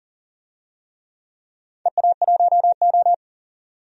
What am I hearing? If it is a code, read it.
Morse code, heard as a keyed tone, about 40 wpm, 700 Hz. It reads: EA1O
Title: DXCC-100 Call Signs - 4 Characters - 40wpm